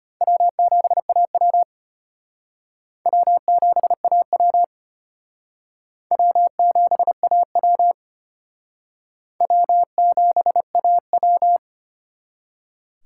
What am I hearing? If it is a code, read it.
Morse code, heard as a keyed tone, about 31 wpm, 700 Hz. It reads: W7AW W7AW W7AW W7AW
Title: Speed Racing - USA 1x2 Call Signs 25wpm